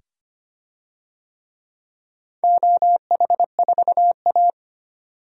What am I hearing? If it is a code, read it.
Morse code, heard as a keyed tone, about 25 wpm, 700 Hz. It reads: OH4A
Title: DXCC-100 Call Signs - 4 Characters - 25wpm